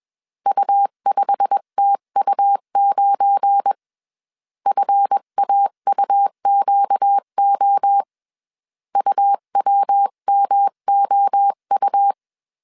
Morse code, heard as a keyed tone, about 21 wpm, 800 Hz. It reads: U5TU9 FAUQO UWMOU
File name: CW / DET / 1 9 1500C S B 20wpm 2313